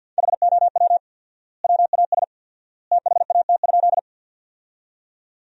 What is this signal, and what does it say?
Morse code, heard as a keyed tone, about 52 wpm, 700 Hz. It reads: HOW WAS THAT?
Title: 4x Word-Spacing - Sentences from Top 100 Words - 50wpm